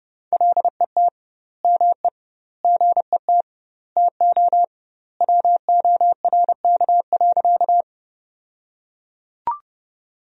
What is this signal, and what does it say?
Morse code, heard as a keyed tone, about 32 wpm, 700 Hz. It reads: LET ME GET TO WORK.
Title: 2x Word-Spacing - Sentences from Top 300 Words - 30wpm